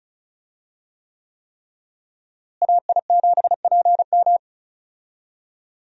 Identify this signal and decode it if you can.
Morse code, heard as a keyed tone, about 36 wpm, 700 Hz. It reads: AI7PM
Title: USA Call Signs - 5 Characters - Repeated 3x - 35wpm